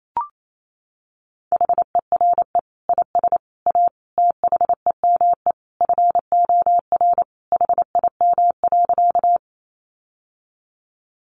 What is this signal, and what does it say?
Morse code, heard as a keyed tone, about 28 wpm, 700 Hz. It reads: HERE IS A THEME FOR HIM.
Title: Sentences from Top 1100 Words 28wpm